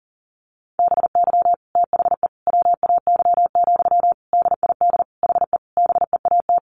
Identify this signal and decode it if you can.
Morse code, heard as a keyed tone, about 40 wpm, 700 Hz. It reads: BY THE WAY, DID HE BEAT
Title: Sentences from Top 1300 Words 40wpm